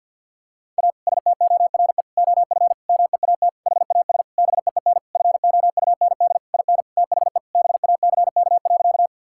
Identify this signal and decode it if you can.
Morse code, heard as a keyed tone, about 50 wpm, 700 Hz. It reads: A STORE OF MEAT HAS BEEN FOUND IN THE BACK.
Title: Sentences from Top 1000 Words - Review Part 5 - 50wpm